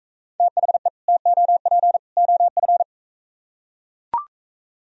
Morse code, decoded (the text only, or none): THE TOP OF